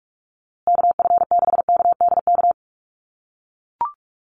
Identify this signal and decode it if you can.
Morse code, heard as a keyed tone, about 44 wpm, 690 Hz. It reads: KF6KDK
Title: USA Call Signs Repeated 3x 45wpm